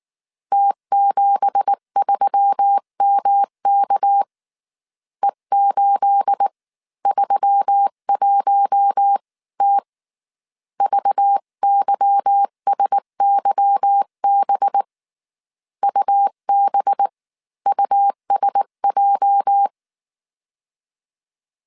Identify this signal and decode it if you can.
Morse code, heard as a keyed tone, about 19 wpm, spaced as for 16 wpm, 800 Hz. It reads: T73MK E831T VYSYB UBUSJ
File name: CW / DET / 1 9 1500C S B 18wpm 2551